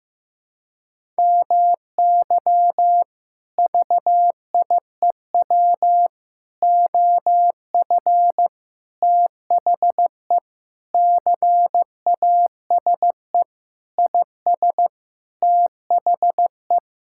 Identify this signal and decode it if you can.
Morse code, heard as a keyed tone, about 14 wpm, 700 Hz. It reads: MY VIEW OF THE CASE IS THE
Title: Sentences from Top 1000 Words - Review Part 6 - 15wpm